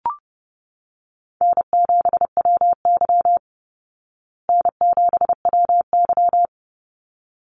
Morse code, heard as a keyed tone, about 29 wpm, 700 Hz. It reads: N7WY N7WY
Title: Call Signs Repeated 3x - 30wpm